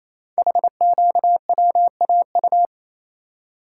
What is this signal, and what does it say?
Morse code, heard as a keyed tone, about 29 wpm, 700 Hz. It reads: HQWAU
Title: Code Groups - 5 Letters - 28wpm